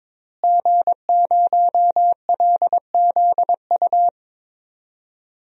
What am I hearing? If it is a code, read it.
Morse code, heard as a keyed tone, about 22 wpm, 700 Hz. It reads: G0LZU
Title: DXCC-100 Call Signs - 5 Characters - Repeated 3x - 22wpm